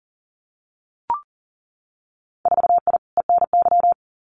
Morse code, heard as a keyed tone, about 40 wpm, 700 Hz. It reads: 4I ENY
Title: Winter Field Day Exchanges 40wpm